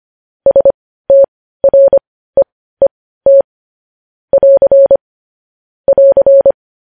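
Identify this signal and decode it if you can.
Morse code, heard as a keyed tone, about 25 wpm, spaced as for 15 wpm, 550 Hz. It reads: STREET <AR> <AR>
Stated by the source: Morse news headlines practice at 15.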